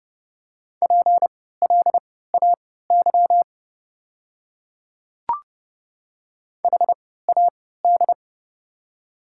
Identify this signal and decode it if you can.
Morse code, heard as a keyed tone, about 30 wpm, spaced as for 17 wpm, 700 Hz. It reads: PLAY HAD
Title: 3x Character Spacing - Top 200 Words - 30wpm